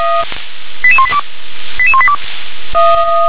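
Five-tone selective call (ZVEI), heard as quick runs of five tones, tones of about 70 ms. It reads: 88182 88182
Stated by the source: ZVEI five-tone call 88052 received via SDR.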